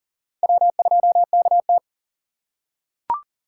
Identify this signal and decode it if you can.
Morse code, heard as a keyed tone, about 40 wpm, 700 Hz. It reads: W2KT
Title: USA Call Signs - 4 Characters - 2x Character Spacing - 40wpm